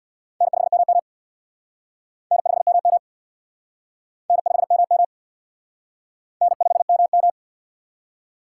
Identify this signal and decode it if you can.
Morse code, heard as a keyed tone, about 60 wpm, 700 Hz. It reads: N5MM N5MM N5MM N5MM
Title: Speed Racing - USA 1x2 Call Signs 50wpm